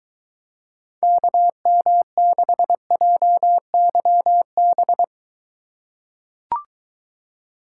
Morse code, heard as a keyed tone, about 23 wpm, 700 Hz. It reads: KM6JYB